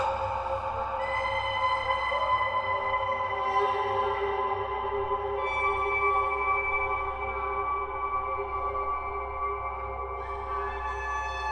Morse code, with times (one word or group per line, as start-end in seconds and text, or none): none